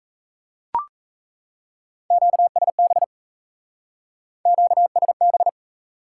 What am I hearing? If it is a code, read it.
Morse code, heard as a keyed tone, about 36 wpm, 700 Hz. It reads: QSB QSB